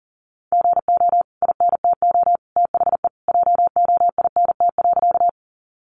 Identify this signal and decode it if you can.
Morse code, heard as a keyed tone, about 40 wpm, 700 Hz. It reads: GO INTO THE JOINT.
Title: Sentences from Top 1300 Words 40wpm